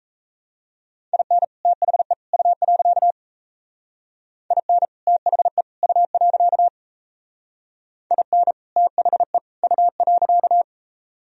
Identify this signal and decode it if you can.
Morse code, heard as a keyed tone, about 34 wpm, 700 Hz. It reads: IN THE U. IN THE U. IN THE U.
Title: Speed-Racing - Sets of 3 Words 28wpm